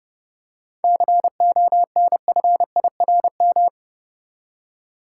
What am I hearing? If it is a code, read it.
Morse code, heard as a keyed tone, about 30 wpm, 700 Hz. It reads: CONFIRM
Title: Seven Letter Words - 30wpm